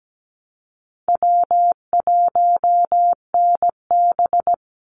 Morse code, heard as a keyed tone, about 17 wpm, 700 Hz. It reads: W1NB
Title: USA Call Signs 17wpm